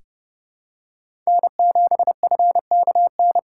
Morse code, heard as a keyed tone, about 30 wpm, 700 Hz. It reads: N7FKN
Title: USA Call Signs - 5 Characters - 4x Character Spacing - 30wpm